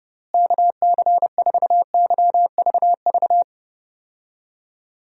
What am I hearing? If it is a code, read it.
Morse code, heard as a keyed tone, about 30 wpm, 700 Hz. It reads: KC4YVV
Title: USA Call Signs - 6 Characters - Repeated 3x - 30wpm